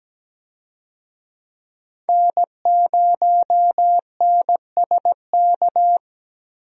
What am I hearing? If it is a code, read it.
Morse code, heard as a keyed tone, about 17 wpm, 700 Hz. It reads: N0NSK
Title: USA Call Signs - 5 Characters - Repeated 2x - 17wpm